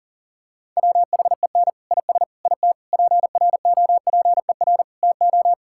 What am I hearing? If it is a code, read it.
Morse code, heard as a keyed tone, about 37 wpm, 700 Hz. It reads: WHEN IS IT PROPER TO
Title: Sentences from Top 1000 Words - Review Part 5 - 40wpm